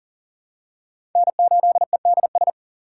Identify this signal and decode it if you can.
Morse code, heard as a keyed tone, about 43 wpm, 700 Hz. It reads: N8EDS
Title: USA Call Signs - 4x Character Spacing - 40wpm